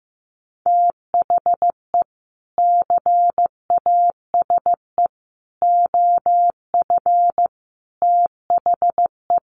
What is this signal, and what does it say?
Morse code, heard as a keyed tone, about 15 wpm, 700 Hz. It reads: THE CASE OF THE
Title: Sets of 4 Words 15wpm